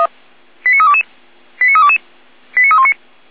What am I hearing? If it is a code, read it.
Five-tone selective call (ZVEI), heard as quick runs of five tones, tones of about 70 ms. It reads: 89322 89322 89319
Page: /455/SDR/ZVEI/89201/